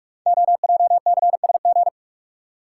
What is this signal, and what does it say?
Morse code, heard as a keyed tone, about 44 wpm, 690 Hz. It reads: OJCSG
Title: Code Groups - 5 Letters - 45wpm